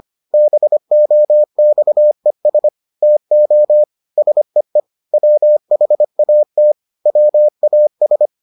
Morse code, heard as a keyed tone, about 25 wpm, 600 Hz. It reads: BOXES TO SEE WHAT WAS